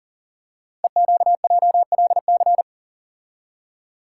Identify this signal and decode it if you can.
Morse code, heard as a keyed tone, about 40 wpm, 700 Hz. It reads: EQJLC
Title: Code Groups - 5 Letter:Numbers - 40wpm